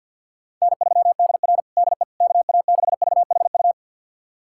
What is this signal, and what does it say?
Morse code, heard as a keyed tone, about 47 wpm, 700 Hz. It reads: N3DR DE KA6VHU
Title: USA Callsign DE Callsign 50wpm